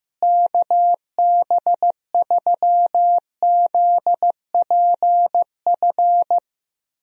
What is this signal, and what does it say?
Morse code, heard as a keyed tone, about 15 wpm, 700 Hz. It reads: KB3ZPF
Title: USA Call Signs - 6 Characters - Repeated 3x - 15wpm